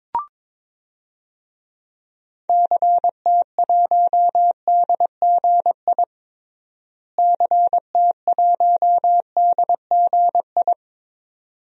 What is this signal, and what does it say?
Morse code, heard as a keyed tone, about 21 wpm, 700 Hz. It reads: CT1DGI CT1DGI
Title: DXCC-100 Call Signs - 6 Characters - Repeated 2x - 22wpm